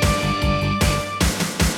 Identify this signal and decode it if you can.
There is no code.